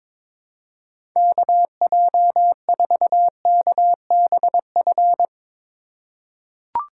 Morse code, heard as a keyed tone, about 23 wpm, 700 Hz. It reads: KJ4KBF
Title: USA Call Signs - 4x Character Spacing - 22wpm